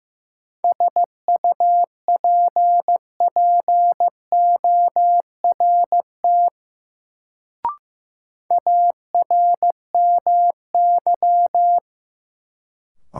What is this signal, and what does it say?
Morse code, heard as a keyed tone, about 16 wpm, 700 Hz. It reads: SUPPORT ARMY
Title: Sentences from Top 800 Words - New Word Drill - 15wpm